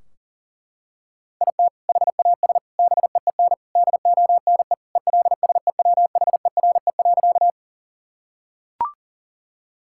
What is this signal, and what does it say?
Morse code, heard as a keyed tone, about 40 wpm, 700 Hz. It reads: IT HAS BEEN DONE ELSEWHERE.
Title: Sentences from Top 1600 Words 40wpm